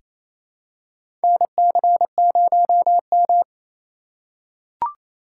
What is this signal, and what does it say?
Morse code, heard as a keyed tone, about 28 wpm, 700 Hz. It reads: NC0M
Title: USA Call Signs - 4 Characters - 6x Character Spacing - 28wpm